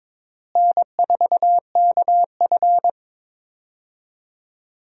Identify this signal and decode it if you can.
Morse code, heard as a keyed tone, about 22 wpm, 700 Hz. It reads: N4KF
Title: USA Call Signs - 4 Characters - Repeated 2x - 22wpm